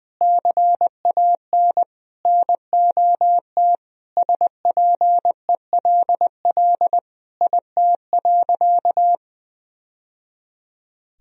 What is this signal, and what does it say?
Morse code, heard as a keyed tone, about 21 wpm, 700 Hz. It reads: CAN NOT SPELL IT.
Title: Sentences from Top 1000 Words - Review Part 1 - 20wpm